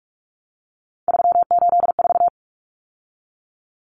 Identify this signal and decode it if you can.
Morse code, heard as a keyed tone, about 44 wpm, 700 Hz. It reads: 384
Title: Numbers - 3-Digit 45wpm